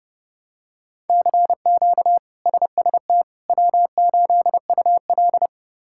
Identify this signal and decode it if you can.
Morse code, heard as a keyed tone, about 30 wpm, 700 Hz. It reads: CQ SST W8UL
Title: K1USN SST QSO Practice - 30wpm